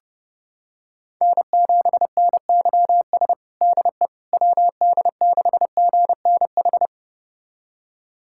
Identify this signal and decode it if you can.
Morse code, heard as a keyed tone, about 30 wpm, 700 Hz. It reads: N7NYS DE WD6GNH
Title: USA Callsign DE Callsign 30wpm